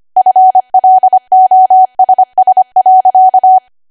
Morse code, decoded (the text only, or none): FLOSS.